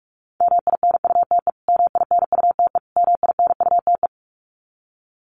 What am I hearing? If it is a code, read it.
Morse code, heard as a keyed tone, about 44 wpm, 700 Hz. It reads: MINUTE MINUTE MINUTE
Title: Top 500 Words Repeated 45wpm